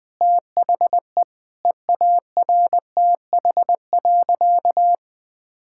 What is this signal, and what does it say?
Morse code, heard as a keyed tone, about 20 wpm, 700 Hz. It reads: THE EARTH.
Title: Difficult Sentences with Hints - 20wpm